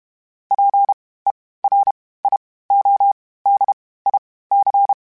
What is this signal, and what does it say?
Morse code, heard as a keyed tone, about 32 wpm, spaced as for 18 wpm, 800 Hz. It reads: PERIODIC
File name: (spaced) / FromSMs spaced mp3s / Antennas sp@30